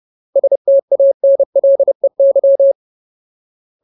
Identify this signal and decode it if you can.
Morse code, heard as a keyed tone, about 30 wpm, 550 Hz. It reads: STANLEY
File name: Ultimate Word List / 30WPM Ultimate